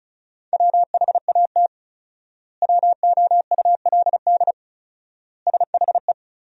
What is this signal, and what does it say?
Morse code, heard as a keyed tone, about 36 wpm, 700 Hz. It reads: WHAT WOULD SHE
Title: 4x Word-Spacing - Sentences from Top 100 Words - 35wpm